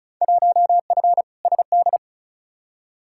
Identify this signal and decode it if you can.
Morse code, heard as a keyed tone, about 36 wpm, 700 Hz. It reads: 1F SD
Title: Field Day Exchanges - 35wpm